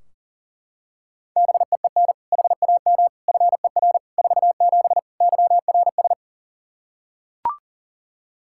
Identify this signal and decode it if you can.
Morse code, heard as a keyed tone, about 40 wpm, 700 Hz. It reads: BEEN HAM FER 47 YRS
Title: Practice QSO - Exchange 3 - More Info - 40wpm